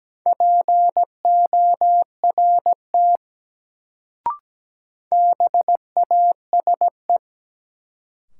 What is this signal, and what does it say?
Morse code, heard as a keyed tone, about 17 wpm, 700 Hz. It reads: PORT BASE
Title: Sentences from Top 1000 Words - Review Part 1 - New Word Drill - 17wpm